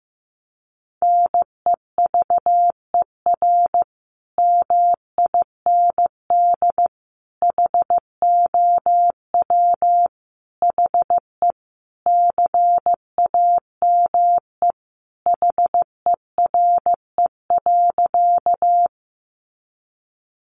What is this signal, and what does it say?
Morse code, heard as a keyed tone, about 15 wpm, 700 Hz. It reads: NEVER MIND HOW HE CAME HERE.
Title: Sentences from Top 500 Words 15wpm